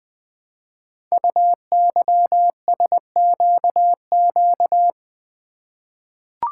Morse code, heard as a keyed tone, about 20 wpm, 700 Hz. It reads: UYSQQ
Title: Code Groups - 5 Letters - 20wpm